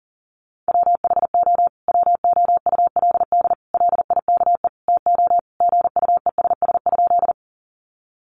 Morse code, heard as a keyed tone, about 40 wpm, 700 Hz. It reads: WHO WOULD LIKE TO GUESS?
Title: Sentences from Top 1500 Words 40wpm